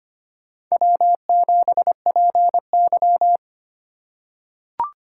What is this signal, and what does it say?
Morse code, heard as a keyed tone, about 25 wpm, 700 Hz. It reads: W7PY
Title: USA Call Signs - 4 Characters - 2x Character Spacing - 25wpm